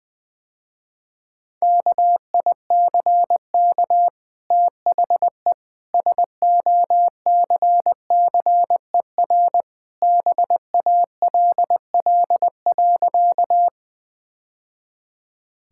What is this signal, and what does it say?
Morse code, heard as a keyed tone, about 21 wpm, 700 Hz. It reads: KICK THE SOCCER BALL.
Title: Easy Sentences with Hints - 20wpm